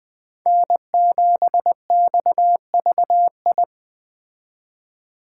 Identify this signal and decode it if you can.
Morse code, heard as a keyed tone, about 20 wpm, 700 Hz. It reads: N7XVI